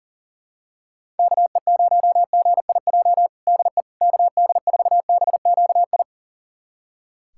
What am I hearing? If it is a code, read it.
Morse code, heard as a keyed tone, about 40 wpm, 700 Hz. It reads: KE0GIJ DE KD4BQI
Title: USA Callsign DE Callsign - 6 Characters - 40wpm